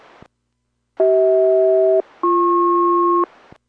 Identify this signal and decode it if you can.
Aircraft SELCAL, heard as two pairs of tones. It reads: CH-BP